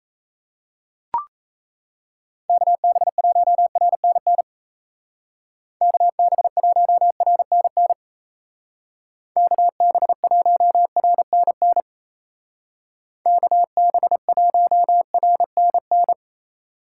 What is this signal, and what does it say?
Morse code, heard as a keyed tone, about 33 wpm, 700 Hz. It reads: KB1RNN KB1RNN KB1RNN KB1RNN
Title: Speed Racing - USA Call Signs 28wpm